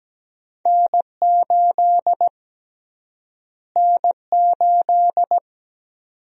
Morse code, heard as a keyed tone, about 17 wpm, 700 Hz. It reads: N8 N8
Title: USA Call Sign Prefixes - Repeated 2x - 17wpm